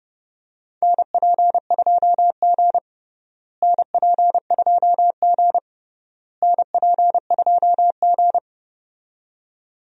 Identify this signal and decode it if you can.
Morse code, heard as a keyed tone, about 29 wpm, 700 Hz. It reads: NP2G NP2G NP2G
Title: USA Call Signs - 4 Characters - Repeated 3x - 30wpm